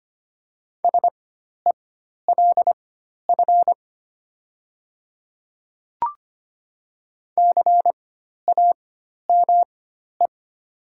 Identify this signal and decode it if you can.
Morse code, heard as a keyed tone, about 25 wpm, spaced as for 12 wpm, 700 Hz. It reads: SELF CAME
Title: 4x Character Spacing - Top 500 Words - 25wpm